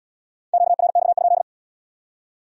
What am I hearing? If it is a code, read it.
Morse code, heard as a keyed tone, about 69 wpm, 700 Hz. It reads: QRX?